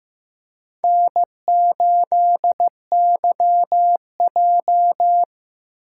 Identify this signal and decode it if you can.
Morse code, heard as a keyed tone, about 15 wpm, 700 Hz. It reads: N8YJ
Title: USA Call Signs - 4 Characters - 3x Character Spacing - 15wpm